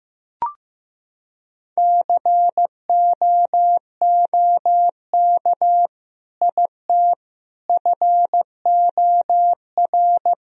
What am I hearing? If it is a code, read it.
Morse code, heard as a keyed tone, about 15 wpm, 700 Hz. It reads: COOK IT FOR
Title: Sentences from Top 1000 Words - Review Part 4 - 15wpm